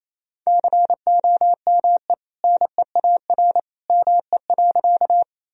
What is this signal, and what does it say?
Morse code, heard as a keyed tone, about 28 wpm, 700 Hz. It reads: COME NEAR ME.
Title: Sentences from Top 1000 Words - Review Part 1 - 28wpm